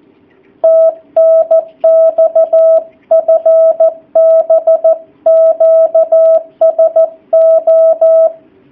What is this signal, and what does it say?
Morse code, heard as a keyed tone, about 14 wpm, 660 Hz. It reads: TNX FB QSO
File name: jh8cbh / 2019cw / sp13- w